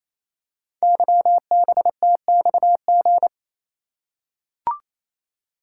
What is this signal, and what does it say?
Morse code, heard as a keyed tone, about 28 wpm, 700 Hz. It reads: YBTXG
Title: Code Groups - 5 Letters - 28wpm